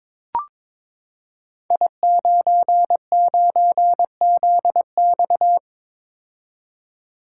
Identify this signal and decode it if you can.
Morse code, heard as a keyed tone, about 22 wpm, 700 Hz. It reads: I99ZX
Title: Code Groups - 5 Letter:Numbers - 22wpm